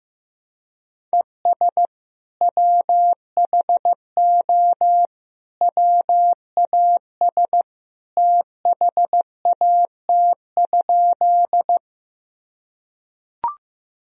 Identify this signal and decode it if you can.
Morse code, heard as a keyed tone, about 15 wpm, 700 Hz. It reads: ES WHO WAS THAT?
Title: Sentences from Top 100 Words and Abbreviations 15wpm